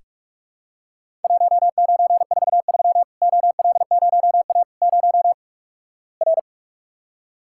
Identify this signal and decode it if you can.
Morse code, heard as a keyed tone, about 46 wpm, 700 Hz. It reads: 1943 OL0A 0 R
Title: Worked All Europe DX Contest - QTC Exchange - 45wpm